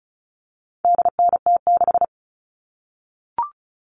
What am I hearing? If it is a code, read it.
Morse code, heard as a keyed tone, about 34 wpm, 700 Hz. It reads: DNT6
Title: All Cut Numbers - 4-Digit 35wpm